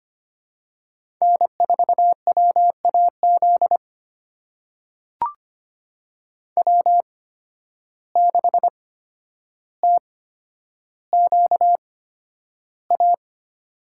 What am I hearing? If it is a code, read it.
Morse code, heard as a keyed tone, about 25 wpm, spaced as for 11 wpm, 700 Hz. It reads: N4WAZ W6TQA
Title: USA Call Signs - 5 Characters - 8x Character Spacing - 25wpm